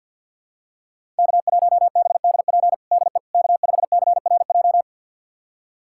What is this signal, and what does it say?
Morse code, heard as a keyed tone, about 52 wpm, 700 Hz. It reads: K1BDP DE K5CRJ
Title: USA Callsign DE Callsign - 5 Characters - 50wpm